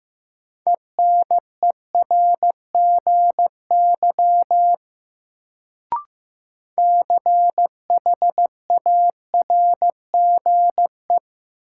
Morse code, heard as a keyed tone, about 15 wpm, 700 Hz. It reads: ENERGY CHARGE
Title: Sentences from Top 900 Words - New Word Drill - 15wpm